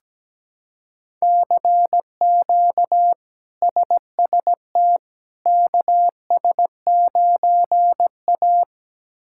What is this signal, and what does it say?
Morse code, heard as a keyed tone, about 17 wpm, 700 Hz. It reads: CQ SST KS9A
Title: K1USN SST QSO Practice - 17wpm